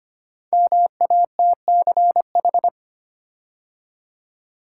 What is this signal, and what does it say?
Morse code, heard as a keyed tone, about 25 wpm, 700 Hz. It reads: MATCH
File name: 5 Letter Words - First 3 Letters Given - 25wpm